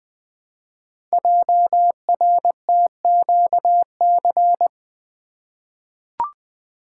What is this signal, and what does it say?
Morse code, heard as a keyed tone, about 20 wpm, 700 Hz. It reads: JRTQC
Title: Code Groups - 5 Letters - 20wpm